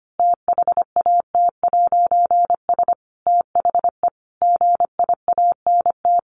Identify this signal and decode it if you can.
Morse code, heard as a keyed tone, about 25 wpm, 700 Hz. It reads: THAT'S THE GIANT
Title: A Sound Of Thunder 25wpm